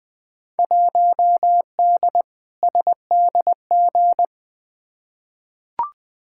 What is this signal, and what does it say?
Morse code, heard as a keyed tone, about 20 wpm, 700 Hz. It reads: 1D SDG
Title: Field Day Exchanges - 20wpm